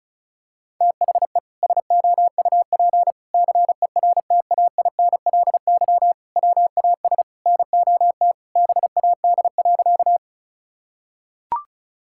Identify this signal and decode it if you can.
Morse code, heard as a keyed tone, about 34 wpm, 700 Hz. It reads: THE SOUP CERTAINLY WAS NOT BAD.
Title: Sentences from Top 1000 Words - Review Part 6 - 35wpm